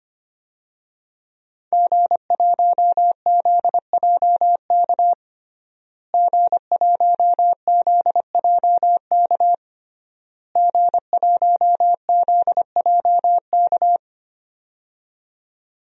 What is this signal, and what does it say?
Morse code, heard as a keyed tone, about 25 wpm, 700 Hz. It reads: G1ZJK G1ZJK G1ZJK
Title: DXCC-100 Call Signs - 5 Characters - Repeated 3x - 25wpm